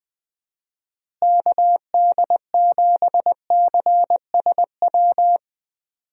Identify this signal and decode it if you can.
Morse code, heard as a keyed tone, about 21 wpm, 700 Hz. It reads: KD7CSW
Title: USA Call Signs - 6 Characters - Repeated 3x - 20wpm